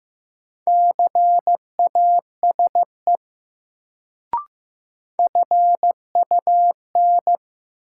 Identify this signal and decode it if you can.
Morse code, heard as a keyed tone, about 15 wpm, 700 Hz. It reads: CASE FUN
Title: Sentences from Top 700 Words - New Word Drill - 15wpm